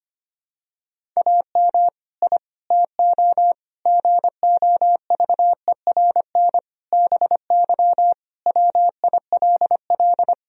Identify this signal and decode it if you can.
Morse code, heard as a keyed tone, about 25 wpm, 700 Hz. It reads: AM I TO GOVERN BY WILL
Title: Sentences from Top 1000 Words - Review Part 2 - 25wpm